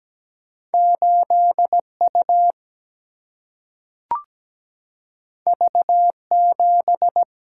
Morse code, heard as a keyed tone, about 17 wpm, 700 Hz. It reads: 8 U V 7
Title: Single-Letter Single-Number 17wpm